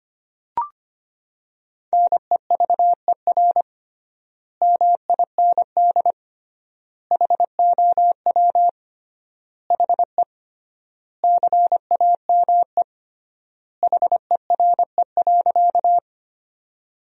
Morse code, heard as a keyed tone, about 25 wpm, 700 Hz. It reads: NEVER MIND HOW HE CAME HERE.
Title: 3x Word-Spacing - Sentences from Top 500 Words - 25wpm